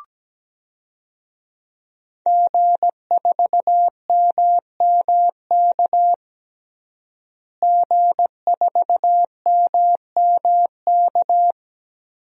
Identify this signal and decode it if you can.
Morse code, heard as a keyed tone, about 17 wpm, 700 Hz. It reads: G4MMK G4MMK
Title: DXCC-100 Call Signs - 5 Characters - Repeated 3x - 17wpm